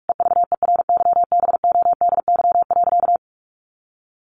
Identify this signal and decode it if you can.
Morse code, heard as a keyed tone, about 44 wpm, 700 Hz. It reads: EVERYBODY.
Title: Sentences from Top 1400 Words 45wpm